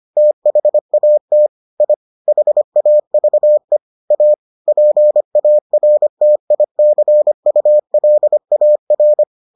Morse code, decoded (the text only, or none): THAT I HAVE A PARTICULAR